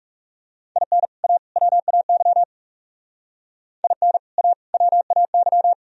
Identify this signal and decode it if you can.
Morse code, heard as a keyed tone, about 36 wpm, 700 Hz. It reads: IN A WAY IN A WAY
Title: Speed-Racing - Sets of 3 Words 30wpm